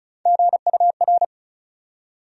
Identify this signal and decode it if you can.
Morse code, heard as a keyed tone, about 36 wpm, 700 Hz. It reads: GUR
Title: USA Call Sign Postfixes - 3 Letters 35wpm